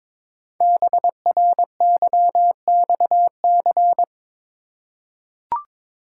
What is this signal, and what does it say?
Morse code, heard as a keyed tone, about 23 wpm, 700 Hz. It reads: BRYXC
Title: Code Groups - 5 Letters - 22wpm